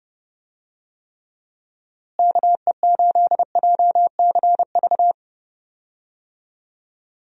Morse code, heard as a keyed tone, about 30 wpm, 700 Hz. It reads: KE8JCV
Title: USA Call Signs - 6 Characters - Repeated 2x - 30wpm